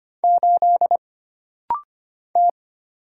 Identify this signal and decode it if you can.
Morse code, heard as a keyed tone, about 25 wpm, 700 Hz. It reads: 8 T
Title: Single Letter-Number - Rapid-Fire 25wpm